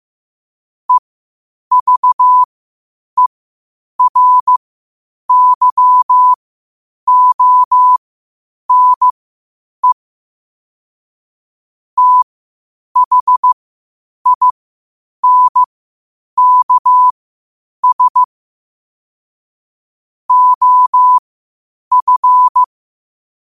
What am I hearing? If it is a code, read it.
Morse code, heard as a keyed tone, about 15 wpm, spaced as for 8 wpm, 1000 Hz. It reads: EVERYONE THINKS OF
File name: qotdfiles / 8wpm